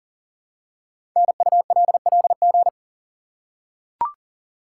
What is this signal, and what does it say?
Morse code, heard as a keyed tone, about 42 wpm, 690 Hz. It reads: NULLG
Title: Code Groups - 5 Letter:Numbers - 40wpm